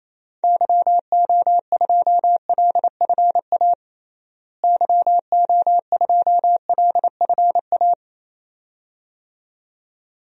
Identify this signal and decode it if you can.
Morse code, heard as a keyed tone, about 28 wpm, 700 Hz. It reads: YO2LFA YO2LFA
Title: DXCC-100 Call Signs - 6 Characters - Repeated 2x - 28wpm